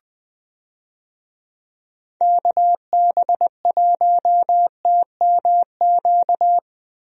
Morse code, heard as a keyed tone, about 20 wpm, 700 Hz. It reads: KB1TMQ
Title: USA Call Signs - 6 Characters - Repeated 2x - 20wpm